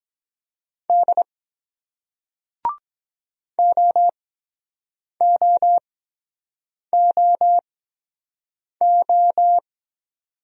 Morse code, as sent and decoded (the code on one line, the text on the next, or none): -.. / --- --- --- ---
D OOOO